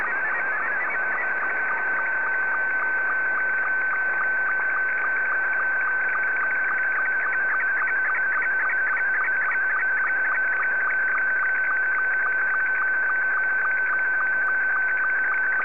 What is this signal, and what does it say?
SSTV (slow-scan television), Scottie 2.
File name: sstv / S2-VK3MCG